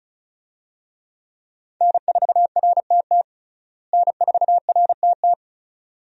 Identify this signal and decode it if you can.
Morse code, heard as a keyed tone, about 34 wpm, 700 Hz. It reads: N4RTT N4RTT
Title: USA Call Signs - 5 Characters - Repeated 2x - 35wpm